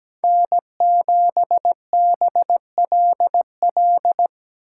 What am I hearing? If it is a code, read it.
Morse code, heard as a keyed tone, about 17 wpm, 700 Hz. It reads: N7BLL
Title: USA Call Signs - 5 Characters - 8x Character Spacing - 17wpm